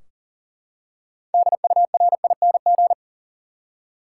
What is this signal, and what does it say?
Morse code, heard as a keyed tone, about 37 wpm, 700 Hz. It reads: DURING
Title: Six Letter Words - 40wpm